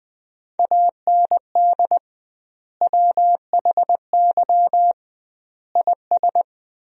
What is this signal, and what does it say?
Morse code, heard as a keyed tone, about 21 wpm, 700 Hz. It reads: AND WHY IS
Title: 2x Word-Spacing - Sentences from Top 200 Words - 20wpm